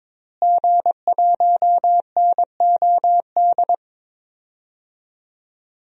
Morse code, heard as a keyed tone, about 22 wpm, 700 Hz. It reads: G1NOD
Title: DXCC-100 Call Signs - 5 Characters - Repeated 2x - 22wpm